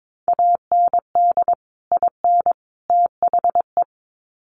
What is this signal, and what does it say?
Morse code, heard as a keyed tone, about 22 wpm, 700 Hz. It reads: AND IN THE
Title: Sets of 3 Words 22wpm